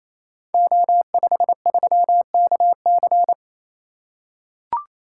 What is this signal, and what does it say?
Morse code, heard as a keyed tone, about 27 wpm, 700 Hz. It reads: O53KC
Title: Code Groups - 5 Letter:Numbers - 28wpm